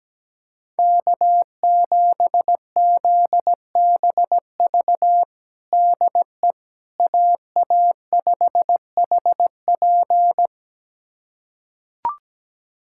Morse code, heard as a keyed tone, about 17 wpm, 700 Hz. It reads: K7ZBV DE AA5HP
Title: USA Callsign DE Callsign - 5 Characters - 17wpm